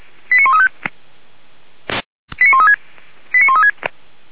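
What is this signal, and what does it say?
Five-tone selective call (ZVEI), heard as quick runs of five tones, tones of about 70 ms. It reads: 89135 89136 89136
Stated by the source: ZVEI five-tone selective call received on SDR.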